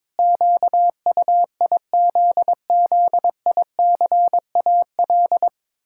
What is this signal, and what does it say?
Morse code, heard as a keyed tone, about 22 wpm, 700 Hz. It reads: QUIZZICAL
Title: Pangram Challenge v2 - 22wpm